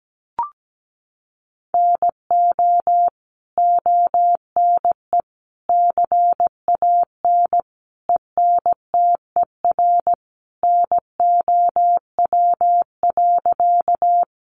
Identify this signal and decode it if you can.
Morse code, heard as a keyed tone, about 17 wpm, 700 Hz. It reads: NO ONE CAN ENTER NOW.
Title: Sentences from Top 1300 Words 17wpm